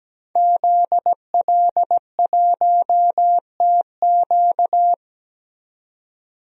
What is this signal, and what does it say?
Morse code, heard as a keyed tone, about 17 wpm, 700 Hz. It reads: ZL1TQ
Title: DXCC-100 Call Signs - 5 Characters - 17wpm